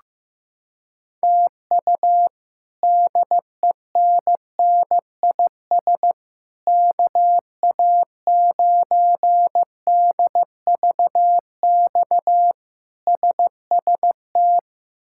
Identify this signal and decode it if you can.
Morse code, heard as a keyed tone, about 15 wpm, 700 Hz. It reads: TU DENNIS KA9DVX SST
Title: K1USN SST QSO Practice - 15wpm